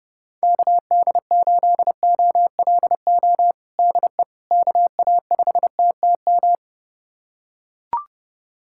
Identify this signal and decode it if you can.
Morse code, heard as a keyed tone, about 30 wpm, 700 Hz. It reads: KD8OLO DE KA5TTM
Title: USA Callsign DE Callsign 30wpm